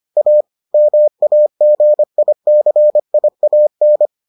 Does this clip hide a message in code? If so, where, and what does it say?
Yes: Morse code from 0.2 to 4.1 s, heard as a keyed tone, about 25 wpm, 600 Hz. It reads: A MAGICIAN